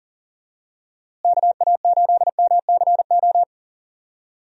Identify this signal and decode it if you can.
Morse code, heard as a keyed tone, about 40 wpm, 700 Hz. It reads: KA8MCO
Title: USA Call Signs - 6 Characters - Repeated 3x - 40wpm